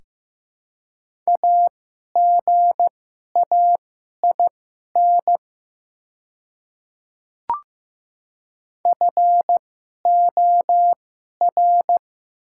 Morse code, heard as a keyed tone, about 15 wpm, spaced as for 11 wpm, 700 Hz. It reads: AGAIN FOR
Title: 2x Character Spacing - Top 300 Words - 15wpm